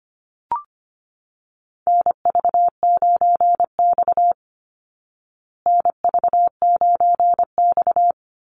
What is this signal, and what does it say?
Morse code, heard as a keyed tone, about 25 wpm, 700 Hz. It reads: NV9X NV9X
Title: Call Signs Repeated 3x - 25wpm